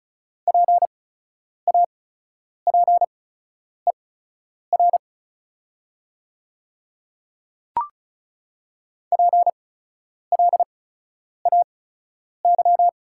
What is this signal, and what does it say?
Morse code, heard as a keyed tone, about 34 wpm, spaced as for 10 wpm, 700 Hz. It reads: PAPER PLAY
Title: 8x Character Spacing - Top 300 Words - 35wpm